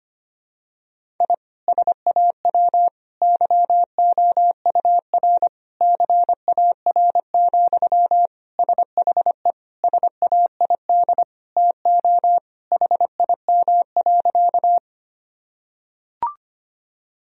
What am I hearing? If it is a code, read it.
Morse code, heard as a keyed tone, about 25 wpm, 700 Hz. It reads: I SAW YOUR CAR, SHE SAID TO HIM.
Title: Sentences from Top 1000 Words - Review Part 1 - 25wpm